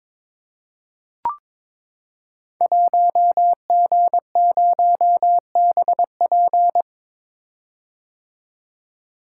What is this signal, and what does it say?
Morse code, heard as a keyed tone, about 22 wpm, 700 Hz. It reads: 1G0BP
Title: Code Groups - 5 Letter:Numbers - 22wpm